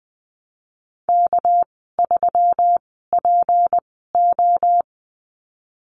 Morse code, heard as a keyed tone, about 20 wpm, spaced as for 15 wpm, 700 Hz. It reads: K3PO
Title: 231 Lesson 37g - Letter x - Call Signs 20wpm